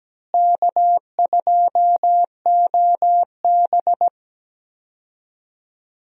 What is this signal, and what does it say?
Morse code, heard as a keyed tone, about 17 wpm, 700 Hz. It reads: K2OB
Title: USA Call Signs - 4 Characters - Repeated 2x - 17wpm